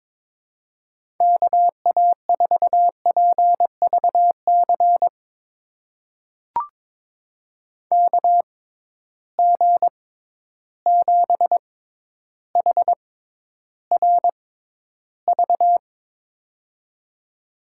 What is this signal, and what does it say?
Morse code, heard as a keyed tone, about 22 wpm, spaced as for 11 wpm, 700 Hz. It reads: KA4PVC KG7HRV
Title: USA Call Signs - 6 Characters - 6x Character Spacing - 22wpm